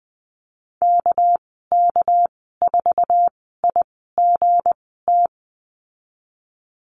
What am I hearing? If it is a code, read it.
Morse code, heard as a keyed tone, about 20 wpm, spaced as for 15 wpm, 700 Hz. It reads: KK4IGT
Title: 231 Lesson 37g - Letter x - Call Signs 20wpm